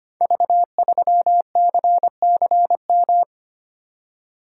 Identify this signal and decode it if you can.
Morse code, heard as a keyed tone, about 25 wpm, 700 Hz. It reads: V3CCM
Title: Code Groups - 5 Letter:Numbers - 25wpm